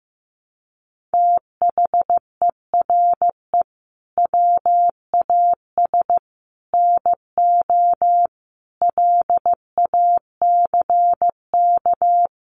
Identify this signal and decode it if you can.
Morse code, heard as a keyed tone, about 15 wpm, 700 Hz. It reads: THERE WAS NO LACK
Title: Sentences from Top 1100 Words 15wpm